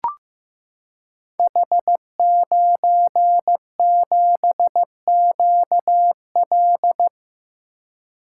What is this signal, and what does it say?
Morse code, heard as a keyed tone, about 15 wpm, 700 Hz. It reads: H97QL
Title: Code Groups - 5 Letter:Numbers - 15wpm